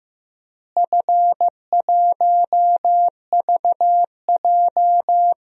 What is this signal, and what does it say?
Morse code, heard as a keyed tone, about 15 wpm, 700 Hz. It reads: F1VJ
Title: DXCC-100 Call Signs - 4 Characters - Repeated 3x - 15wpm